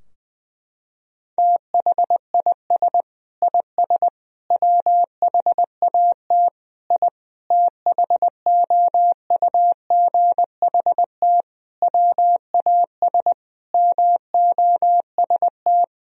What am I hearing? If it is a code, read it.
Morse code, heard as a keyed tone, about 20 wpm, 700 Hz. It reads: THIS IS WHAT I THOUGHT WAS MOST
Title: Sentences from Top 1000 Words - Review Part 4 - 20wpm